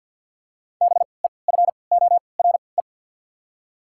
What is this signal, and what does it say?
Morse code, heard as a keyed tone, about 53 wpm, spaced as for 29 wpm, 700 Hz. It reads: BEFORE
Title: 3x Character Spacing - Top 200 Words - 50wpm